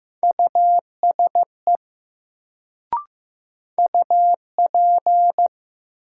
Morse code, heard as a keyed tone, about 15 wpm, 700 Hz. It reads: USE UP
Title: Sets of 2 Words - New Word Drill - 15wpm